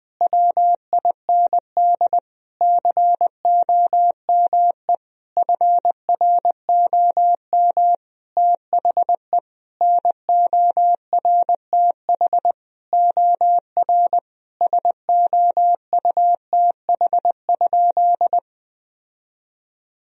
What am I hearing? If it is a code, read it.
Morse code, heard as a keyed tone, about 20 wpm, 700 Hz. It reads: WIND COME FROM THE NORTH OR SOUTH?
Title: Sentences from Top 1000 Words - Review Part 2 - 20wpm